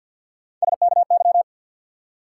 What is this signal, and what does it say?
Morse code, heard as a keyed tone, about 54 wpm, 700 Hz. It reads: SKY
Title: Three Letter Words - 50wpm